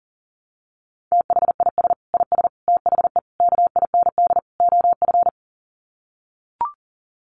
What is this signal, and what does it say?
Morse code, heard as a keyed tone, about 40 wpm, 700 Hz. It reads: THIS IS THE KIND OF
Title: Sets of 5 Words Encore 40wpm